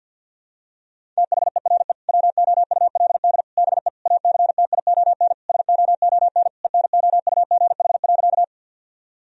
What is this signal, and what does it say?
Morse code, heard as a keyed tone, about 50 wpm, 700 Hz. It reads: THERE WOULD BE ACTION SOON ENOUGH.
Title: Sentences from Top 1000 Words - Review Part 4 - 50wpm